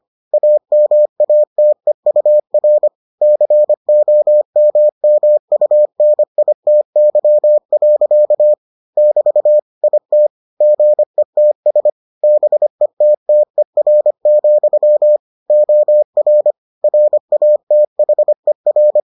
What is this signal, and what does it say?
Morse code, heard as a keyed tone, about 25 wpm, 600 Hz. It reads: AMATEUR COMMUNITY. <BT> IT GETS BETTER, OR RATHER